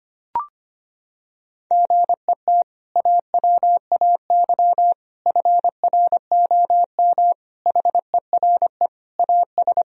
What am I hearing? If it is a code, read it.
Morse code, heard as a keyed tone, about 25 wpm, 700 Hz. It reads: GET AWAY FROM HERE AS
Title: Sentences from Top 1000 Words - Review Part 2 - 25wpm